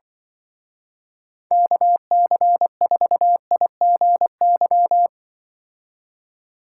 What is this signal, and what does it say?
Morse code, heard as a keyed tone, about 23 wpm, 700 Hz. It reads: KC4IGY
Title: Speed Racing - USA Call Signs 20wpm